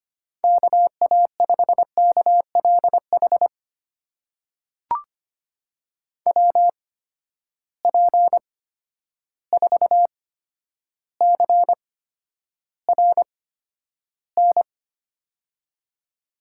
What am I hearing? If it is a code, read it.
Morse code, heard as a keyed tone, about 25 wpm, spaced as for 11 wpm, 700 Hz. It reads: KA5KLH WP4CRN
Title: USA Call Signs - 6 Characters - 8x Character Spacing - 25wpm